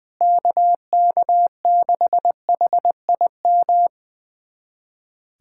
Morse code, heard as a keyed tone, about 21 wpm, 700 Hz. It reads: KK6HIM
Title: USA Callsign DE Callsign - 6 Characters - 20wpm